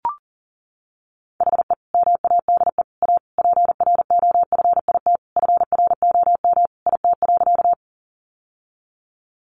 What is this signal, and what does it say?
Morse code, heard as a keyed tone, about 37 wpm, 700 Hz. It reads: HE MADE A PROFIT FROM IT.
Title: Sentences from Top 800 Words 40wpm